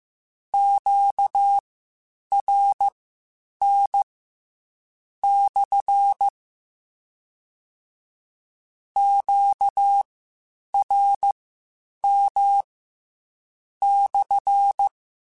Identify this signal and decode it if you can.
Morse code, heard as a keyed tone, about 15 wpm, spaced as for 7 wpm, 800 Hz. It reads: QRN/ QRM/
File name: Q Calls / Q Calls-mp3s / Q CALLS SPSL1 @13